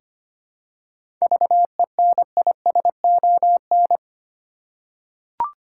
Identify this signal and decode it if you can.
Morse code, heard as a keyed tone, about 25 wpm, 700 Hz. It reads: VENISON
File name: practice / 25wpm / Difficult Words with Hints - 25wpm